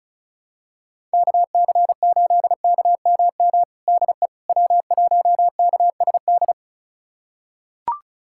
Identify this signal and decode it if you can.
Morse code, heard as a keyed tone, about 36 wpm, 700 Hz. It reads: KC8KMM DE W1KSD
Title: USA Callsign DE Callsign 35wpm